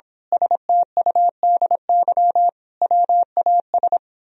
Morse code, heard as a keyed tone, about 26 wpm, 700 Hz. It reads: STUDY WAS
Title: Speed-Racing - Sets of 5 Words 22wpm